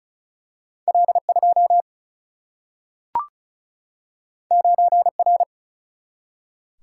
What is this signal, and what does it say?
Morse code, heard as a keyed tone, about 36 wpm, 700 Hz. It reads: L2 9R
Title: Single-Letter Single-Number 35wpm